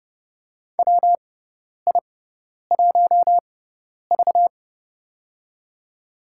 Morse code, heard as a keyed tone, about 30 wpm, spaced as for 10 wpm, 700 Hz. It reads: WI1V
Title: USA Call Signs - 6x Character Spacing - 30wpm